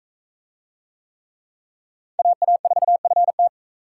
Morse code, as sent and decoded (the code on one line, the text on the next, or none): .- .- ....- ..-. -
AA4FT